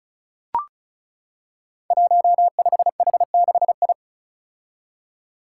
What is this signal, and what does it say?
Morse code, heard as a keyed tone, about 35 wpm, 700 Hz. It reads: 15H6I
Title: Five Challenging Letter-Number Combinations 35wpm